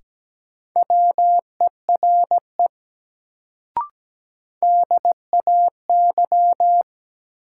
Morse code, heard as a keyed tone, about 17 wpm, 700 Hz. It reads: WERE DAY
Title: Sets of 2 Words - New Word Drill - 17wpm